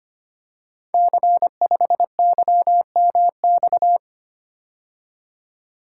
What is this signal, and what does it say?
Morse code, heard as a keyed tone, about 25 wpm, 700 Hz. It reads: C5YMX
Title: Code Groups - 5 Letter:Numbers - 25wpm